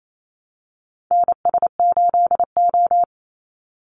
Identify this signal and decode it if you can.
Morse code, heard as a keyed tone, about 28 wpm, 700 Hz. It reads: NS8O
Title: Call Signs - 28wpm